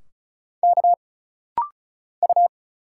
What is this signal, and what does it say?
Morse code, heard as a keyed tone, about 34 wpm, 700 Hz. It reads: K U
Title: Single Letters - Rapid-Fire 35wpm